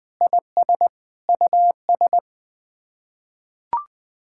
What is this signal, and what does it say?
Morse code, heard as a keyed tone, about 21 wpm, 700 Hz. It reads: IS US
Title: Sets of 9 Words - 20wpm